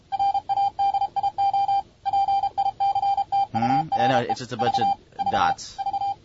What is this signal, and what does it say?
Morse code, heard as a keyed tone, about 33 wpm, 790 Hz, with speech and radio noise. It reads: RADIO PICTURE V V V